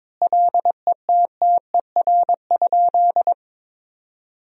Morse code, heard as a keyed tone, about 22 wpm, 700 Hz. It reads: LETTER?
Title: 2x Word-Spacing - Sentences from Top 300 Words - 22wpm